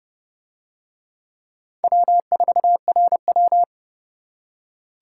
Morse code, heard as a keyed tone, about 29 wpm, 700 Hz. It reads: W4RW